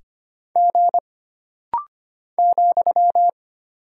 Morse code, heard as a keyed tone, about 25 wpm, 700 Hz. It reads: G ,
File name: Single Character (Letter:Number:Punctuation:Prosign) - Rapid-Fire - 25wpm